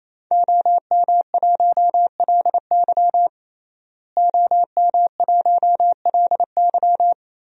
Morse code, heard as a keyed tone, about 28 wpm, 700 Hz. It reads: OM1LY OM1LY
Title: DXCC-100 Call Signs - 5 Characters - Repeated 3x - 28wpm